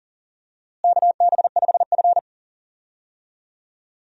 Morse code, heard as a keyed tone, about 40 wpm, 700 Hz. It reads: KB5F
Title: USA Call Signs - 4 Characters - Repeated 3x - 40wpm